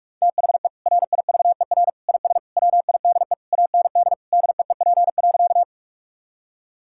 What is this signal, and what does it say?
Morse code, heard as a keyed tone, about 46 wpm, 700 Hz. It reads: THE RIVER IS WIDE AND DEEP.
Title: Sentences from Top 1000 Words - Review Part 6 - 45wpm